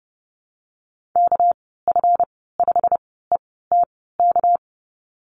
Morse code, heard as a keyed tone, about 30 wpm, spaced as for 17 wpm, 700 Hz. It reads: KF5ETK
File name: Lesson 37g - Letter x - Call Signs 30wpm